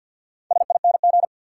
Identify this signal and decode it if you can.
Morse code, heard as a keyed tone, about 52 wpm, 700 Hz. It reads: SING
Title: Sentences from Top 400 Words - New Word Drill - 50wpm